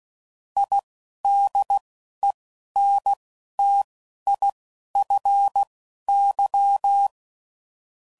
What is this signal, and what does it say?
Morse code, heard as a keyed tone, about 16 wpm, spaced as for 12 wpm, 800 Hz. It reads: IDENTIFY
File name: CW Code Groups @10WPM / Technical Subjects / SwPwrSup